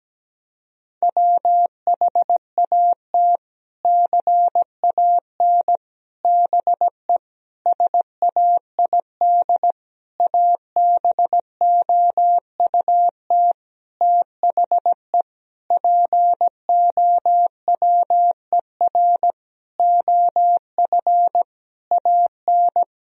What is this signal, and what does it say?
Morse code, heard as a keyed tone, about 17 wpm, 700 Hz. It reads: WHAT CAN BE SAID ABOUT THE POWER OF AN
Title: Sentences from Top 1000 Words - Review Part 4 - 17wpm